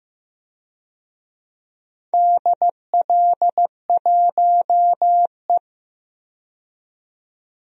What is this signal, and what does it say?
Morse code, heard as a keyed tone, about 15 wpm, 700 Hz. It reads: DL1E
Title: DXCC-100 Call Signs - 4 Characters - 15wpm